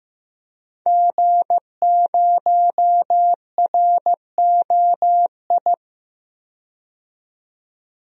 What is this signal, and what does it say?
Morse code, heard as a keyed tone, about 15 wpm, 700 Hz. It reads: G0ROI
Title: DXCC-100 Call Signs - 5 Characters - 15wpm